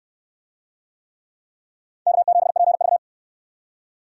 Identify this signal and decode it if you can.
Morse code, heard as a keyed tone, about 68 wpm, 700 Hz. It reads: K7PV